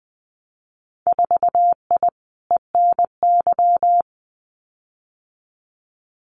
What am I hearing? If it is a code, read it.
Morse code, heard as a keyed tone, about 19 wpm, 700 Hz. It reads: 4I ENY
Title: Winter Field Day Exchanges 20wpm